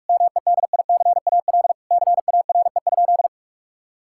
Morse code, heard as a keyed tone, about 46 wpm, 700 Hz. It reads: MEDICAL CARE?
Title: Sentences from Top 1000 Words - Review Part 5 - 45wpm